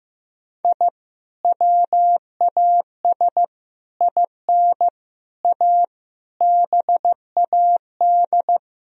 Morse code, heard as a keyed tone, about 15 wpm, 700 Hz. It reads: I WAS IN A BAD
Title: Sentences from Top 1000 Words - Review Part 2 - 15wpm